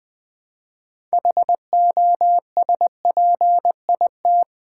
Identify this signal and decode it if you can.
Morse code, heard as a keyed tone, about 20 wpm, 700 Hz. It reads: HOSPIT
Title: Build A Word - 2 Letters At A Time - 8-Letter Words - 20wpm